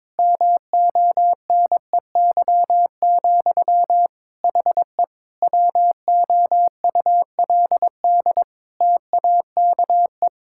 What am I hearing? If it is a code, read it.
Morse code, heard as a keyed tone, about 22 wpm, 700 Hz. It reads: MONEY, HE WOULD TAKE
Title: Sentences from Top 1000 Words - Review Part 6 - 22wpm